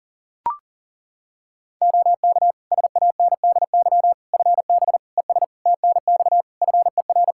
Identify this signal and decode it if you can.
Morse code, heard as a keyed tone, about 40 wpm, 700 Hz. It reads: OK SANDY FB ES TNX FER
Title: K1USN SST QSO Practice 40wpm